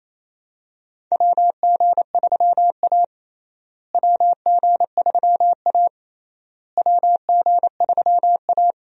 Morse code, heard as a keyed tone, about 27 wpm, 700 Hz. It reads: WG3A WG3A WG3A
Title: USA Call Signs - 4 Characters - Repeated 3x - 28wpm